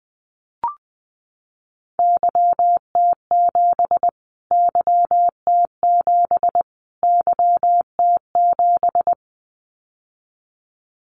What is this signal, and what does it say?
Morse code, heard as a keyed tone, about 20 wpm, 700 Hz. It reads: YT7 YT7 YT7
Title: Callsign Prefixes for Top 25 DXCC - Repeated 3x - 20wpm